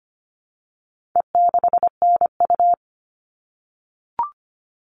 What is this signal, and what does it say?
Morse code, heard as a keyed tone, about 25 wpm, 700 Hz. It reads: E6NU
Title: All Cut Numbers - 4-Digit 25wpm